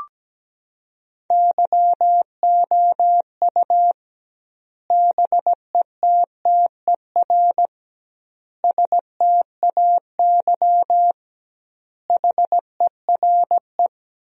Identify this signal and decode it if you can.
Morse code, heard as a keyed tone, about 17 wpm, 700 Hz. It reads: YOU BETTER STAY HERE
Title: 2x Word-Spacing - Sentences from Top 500 Words - 17wpm